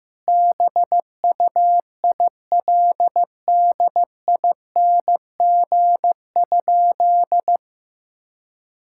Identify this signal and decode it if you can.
Morse code, heard as a keyed tone, about 15 wpm, 700 Hz. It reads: BUILDING?
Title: Sentences from Top 1000 Words - Review Part 5 - 15wpm